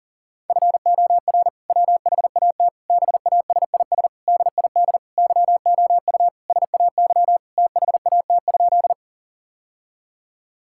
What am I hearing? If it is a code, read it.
Morse code, heard as a keyed tone, about 40 wpm, 700 Hz. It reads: FOR WHAT BASIS DID YOU SAY THAT?
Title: Sentences from Top 1000 Words - Review Part 5 - 40wpm